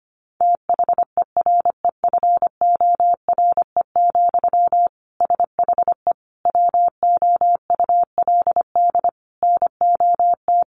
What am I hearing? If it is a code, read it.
Morse code, heard as a keyed tone, about 25 wpm, 700 Hz. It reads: THEREFORE, SHE WOULD NOT
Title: Sentences from Top 800 Words 25wpm